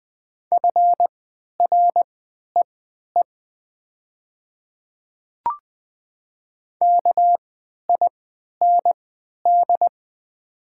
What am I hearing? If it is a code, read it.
Morse code, heard as a keyed tone, about 20 wpm, spaced as for 11 wpm, 700 Hz. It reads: FREE KIND
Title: 3x Character Spacing - Top 500 Words - 20wpm